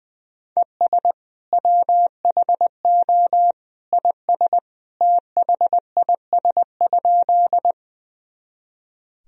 Morse code, heard as a keyed tone, about 20 wpm, 700 Hz. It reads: ES WHO IS THIS?
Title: Sentences from Top 100 Words and Abbreviations 20wpm